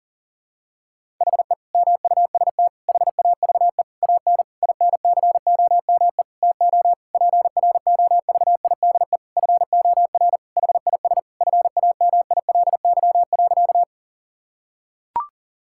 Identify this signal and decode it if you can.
Morse code, heard as a keyed tone, about 37 wpm, 700 Hz. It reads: HE MUST HAVE AN INCOME TO PROVIDE FOR HIS FAMILY.
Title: Sentences from Top 1000 Words - Review Part 5 - 40wpm